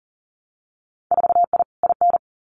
Morse code, heard as a keyed tone, about 42 wpm, 700 Hz. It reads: 4I IN
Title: Winter Field Day Exchanges 40wpm